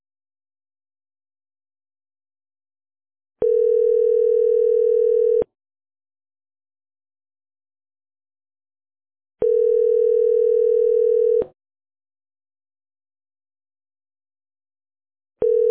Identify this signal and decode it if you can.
Ringback tone (North America), 440 + 480 Hz.